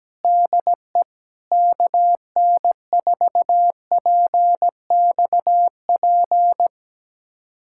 Morse code, heard as a keyed tone, about 17 wpm, 700 Hz. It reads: DE KN4PXP
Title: USA Callsign DE Callsign 17wpm